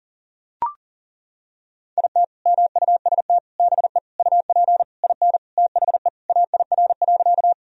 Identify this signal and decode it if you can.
Morse code, heard as a keyed tone, about 40 wpm, 700 Hz. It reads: IT MUST BE UP IN THE AIR.
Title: Sentences from Top 1000 Words - Review Part 1 - 40wpm